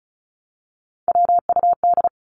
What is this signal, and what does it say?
Morse code, heard as a keyed tone, about 34 wpm, 700 Hz. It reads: WUD
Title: Abbreviations Repeated 3x - 35wpm